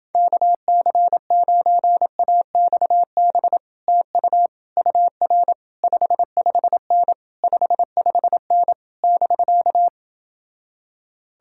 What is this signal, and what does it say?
Morse code, heard as a keyed tone, about 30 wpm, 700 Hz. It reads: KC9AXB TU UR 55N 55N <BK>